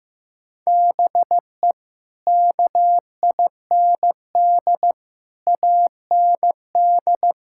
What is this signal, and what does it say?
Morse code, heard as a keyed tone, about 15 wpm, 700 Hz. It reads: BE KIND AND
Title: Sentences from Top 1000 Words - Review Part 5 - 15wpm